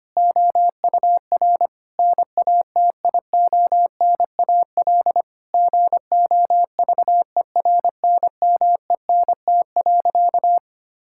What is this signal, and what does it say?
Morse code, heard as a keyed tone, about 25 wpm, 700 Hz. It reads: OUR NATIONAL GOVERNMENT.
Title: Sentences from Top 1000 Words - Review Part 5 - 25wpm